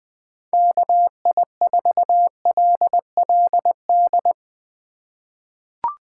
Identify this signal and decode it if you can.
Morse code, heard as a keyed tone, about 21 wpm, 700 Hz. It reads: KI4LLD
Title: USA Call Signs - 3x Character Spacing - 20wpm